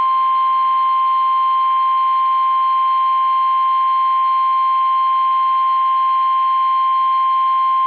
Emergency Alert System (SAME) tones (the 1050 Hz attention tone). It is heard from 0.3 to 7.0 s.
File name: eas / KUSO / KUSO